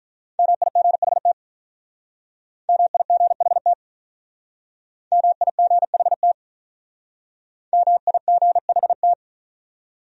Morse code, heard as a keyed tone, about 42 wpm, 700 Hz. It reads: MIGHT MIGHT MIGHT MIGHT
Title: Speed-Racing Top 300 Words 35wpm